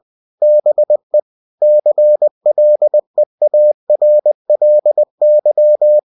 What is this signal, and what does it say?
Morse code, heard as a keyed tone, about 20 wpm, 600 Hz. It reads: BE CLEARLY
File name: cw-books / winnie-the-pooh / winnie-20wpm-page-0117